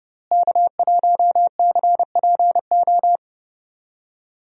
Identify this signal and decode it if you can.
Morse code, heard as a keyed tone, about 30 wpm, 700 Hz. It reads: K1CPO
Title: USA Call Signs - 5 Characters - Repeated 2x - 30wpm